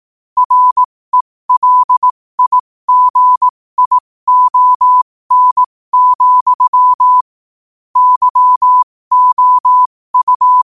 Morse code, heard as a keyed tone, about 18 wpm, spaced as for 15 wpm, 1000 Hz. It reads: RELIGION, YOU
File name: qotdfiles / 15wpm